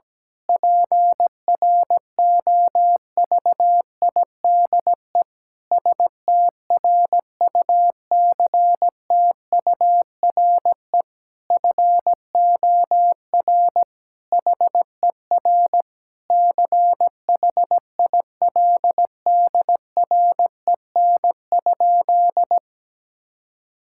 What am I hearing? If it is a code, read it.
Morse code, heard as a keyed tone, about 17 wpm, 700 Hz. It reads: PROVIDE STRUCTURE FOR HER CHILDREN?
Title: Sentences from Top 1000 Words - Review Part 6 - 17wpm